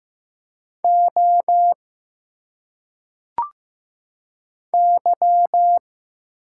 Morse code, heard as a keyed tone, about 15 wpm, 700 Hz. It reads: O Y
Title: Single Letters - Phonetic Pronunciation 15wpm